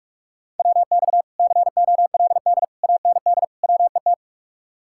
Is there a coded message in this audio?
Yes: Morse code from 0.6 to 4.1 s, heard as a keyed tone, about 44 wpm, 700 Hz. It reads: WX COLD AND WET